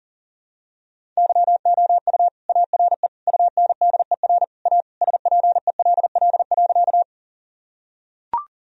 Morse code, heard as a keyed tone, about 40 wpm, 700 Hz. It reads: YOU ARE UNDER A SPELL.
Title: Sentences from Top 1000 Words - Review Part 1 - 40wpm